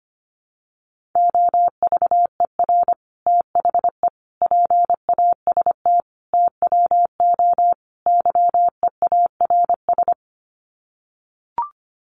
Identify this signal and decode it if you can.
Morse code, heard as a keyed tone, about 25 wpm, 700 Hz. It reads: OVER THE PAST TWO YEARS
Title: Sets of 5 Words Encore 25wpm